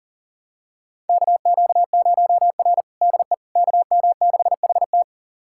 Morse code, heard as a keyed tone, about 40 wpm, 700 Hz. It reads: KQ0R DE KM6HT
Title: USA Callsign DE Callsign 40wpm